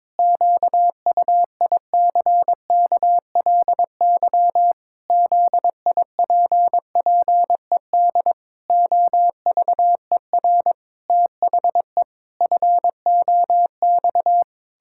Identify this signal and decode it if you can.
Morse code, heard as a keyed tone, about 22 wpm, 700 Hz. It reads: QUICKLY ZIPPED OVER THE FOX
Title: Pangram Challenge v2 - 22wpm